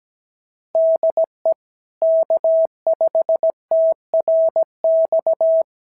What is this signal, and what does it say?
Morse code, heard as a keyed tone, about 17 wpm, 650 Hz. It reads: DE K5TRX
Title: SOTA QSO Practice 17wpm